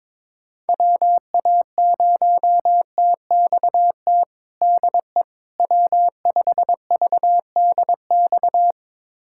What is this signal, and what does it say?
Morse code, heard as a keyed tone, about 22 wpm, 700 Hz. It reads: WA0TXT DE W5VDX
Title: USA Callsign DE Callsign 22wpm